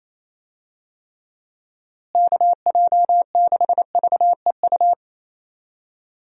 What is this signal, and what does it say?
Morse code, heard as a keyed tone, about 28 wpm, 700 Hz. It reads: KJ6VEU
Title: USA Call Signs - 6 Characters - Repeated 3x - 28wpm